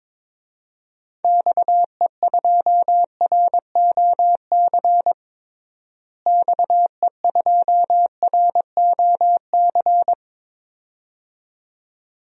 Morse code, heard as a keyed tone, about 22 wpm, 700 Hz. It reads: XE2ROC XE2ROC
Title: DXCC-100 Call Signs - 6 Characters - Repeated 2x - 22wpm